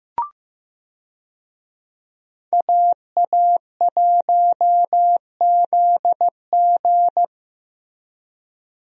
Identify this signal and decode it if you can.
Morse code, heard as a keyed tone, about 15 wpm, 700 Hz. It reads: AA1ZG
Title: USA Call Signs - 5 Characters - Repeated 3x - 15wpm